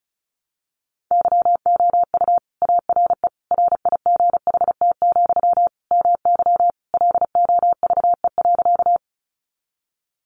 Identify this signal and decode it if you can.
Morse code, heard as a keyed tone, about 34 wpm, 700 Hz. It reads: YOU ARE RIGHT, MY LOVE.
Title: Sentences from Top 400 Words 35wpm